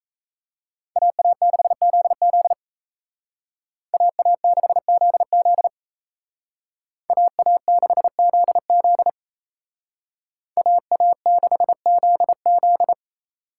Morse code, heard as a keyed tone, about 34 wpm, 700 Hz. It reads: AA6ZZ AA6ZZ AA6ZZ AA6ZZ
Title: Speed Racing - USA 2x2 Call Signs 28wpm